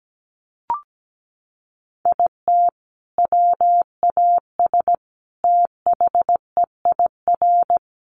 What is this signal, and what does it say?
Morse code, heard as a keyed tone, about 17 wpm, 700 Hz. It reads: IT WAS THEIR
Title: Sentences from Top 1500 Words 17wpm